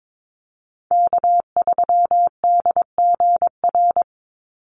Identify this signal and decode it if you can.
Morse code, heard as a keyed tone, about 22 wpm, 700 Hz. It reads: K3DGR
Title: Call Signs - 22wpm